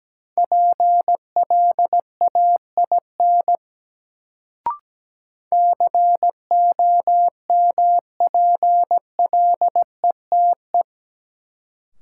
Morse code, heard as a keyed tone, about 17 wpm, 700 Hz. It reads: PLAIN COMPLETE
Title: Sentences from Top 400 Words - New Word Drill - 17wpm